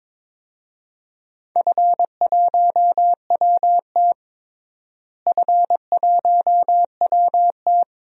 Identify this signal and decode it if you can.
Morse code, heard as a keyed tone, about 22 wpm, 700 Hz. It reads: F1WT F1WT
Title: DXCC-100 Call Signs - 4 Characters - Repeated 3x - 22wpm